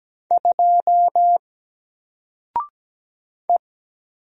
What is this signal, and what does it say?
Morse code, heard as a keyed tone, about 17 wpm, 700 Hz. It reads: 2 E
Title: Single Letter-Number Alt 17wpm